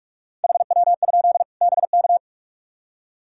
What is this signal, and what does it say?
Morse code, heard as a keyed tone, about 46 wpm, 700 Hz. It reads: HW? BK
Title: K1USN SST QSO Practice 45wpm